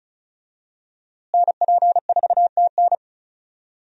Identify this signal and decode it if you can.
Morse code, heard as a keyed tone, about 34 wpm, 700 Hz. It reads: NP4TN
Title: USA Call Signs - 5 Characters - Repeated 2x - 35wpm